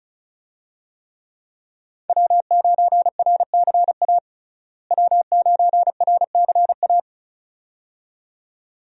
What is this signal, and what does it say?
Morse code, heard as a keyed tone, about 34 wpm, 700 Hz. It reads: W9RCA W9RCA
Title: USA Call Signs - 5 Characters - Repeated 2x - 35wpm